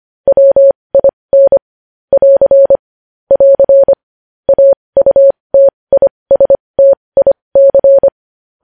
Morse code, heard as a keyed tone, about 25 wpm, spaced as for 20 wpm, 550 Hz. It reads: WIN <AR> <AR> AUTISTIC
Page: news headlines in morse code at 20 wpm